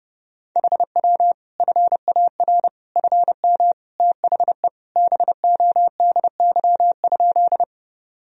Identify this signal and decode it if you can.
Morse code, heard as a keyed tone, about 30 wpm, 700 Hz. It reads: HW FAR FM THE BODY?
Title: Sentences from Top 400 Words and Abbreviations 30wpm